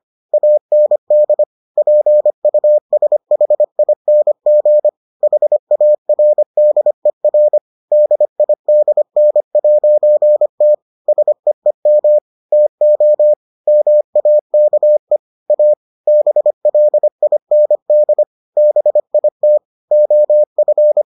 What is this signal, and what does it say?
Morse code, heard as a keyed tone, about 25 wpm, 600 Hz. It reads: AND PUSHING HARDER DIDN'T SEEM TO MAKE A BLIND BIT OF